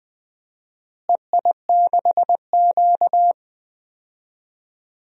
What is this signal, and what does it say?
Morse code, heard as a keyed tone, about 21 wpm, 700 Hz. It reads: EI6Q
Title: DXCC-100 Call Signs - 4 Characters - Repeated 3x - 20wpm